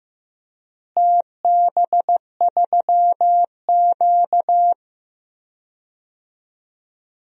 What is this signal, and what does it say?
Morse code, heard as a keyed tone, about 15 wpm, 700 Hz. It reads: TB3Q
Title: DXCC-100 Call Signs - 4 Characters - Repeated 2x - 15wpm